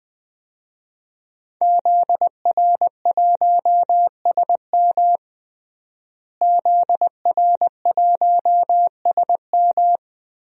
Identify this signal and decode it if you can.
Morse code, heard as a keyed tone, about 20 wpm, 700 Hz. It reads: ZR1SM ZR1SM
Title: DXCC-100 Call Signs - 5 Characters - Repeated 3x - 20wpm